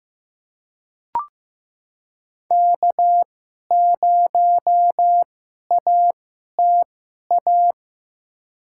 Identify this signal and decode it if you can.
Morse code, heard as a keyed tone, about 15 wpm, spaced as for 11 wpm, 700 Hz. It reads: K0ATA
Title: USA Call Signs - 5 Characters - 2x Character Spacing - 15wpm